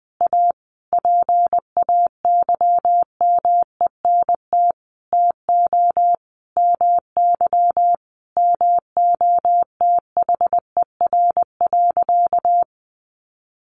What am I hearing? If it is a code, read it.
Morse code, heard as a keyed tone, about 20 wpm, 700 Hz. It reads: A PAYMENT TO MY MOTHER.
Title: Sentences from Top 1400 Words 20wpm